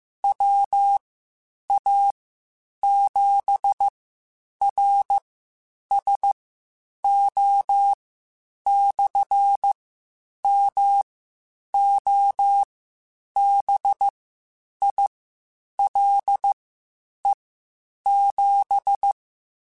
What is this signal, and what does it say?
Morse code, heard as a keyed tone, about 15 wpm, spaced as for 8 wpm, 800 Hz. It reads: WA7RSO/MOBILE7
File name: (spaced) / QSOs Spaced-mp3s / WA7QOX 1 @13 sp